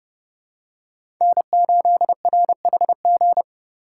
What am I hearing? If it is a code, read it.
Morse code, heard as a keyed tone, about 30 wpm, 700 Hz. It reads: N8RHG
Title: USA Call Signs - 5 Characters - 3x Character Spacing - 30wpm